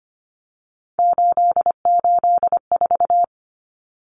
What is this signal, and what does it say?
Morse code, heard as a keyed tone, about 25 wpm, 700 Hz. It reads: 884
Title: Common Cut Numbers - 3-Digit 25wpm